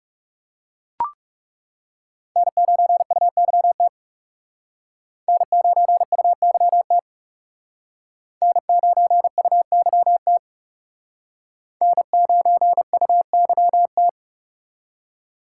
Morse code, heard as a keyed tone, about 36 wpm, 700 Hz. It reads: N9UYT N9UYT N9UYT N9UYT
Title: Speed Racing - USA Call Signs 30wpm